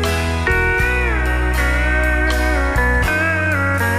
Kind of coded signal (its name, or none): Emergency Alert System (SAME) tones